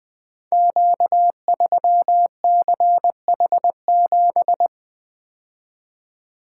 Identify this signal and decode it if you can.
Morse code, heard as a keyed tone, about 20 wpm, 700 Hz. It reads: Q3CH7
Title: Code Groups - 5 Letter:Numbers - 20wpm